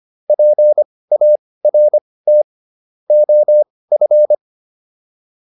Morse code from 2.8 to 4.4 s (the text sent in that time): OF